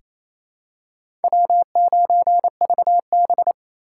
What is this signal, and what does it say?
Morse code, heard as a keyed tone, about 27 wpm, 700 Hz. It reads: W9VB
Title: USA Call Signs - 4 Characters - 4x Character Spacing - 28wpm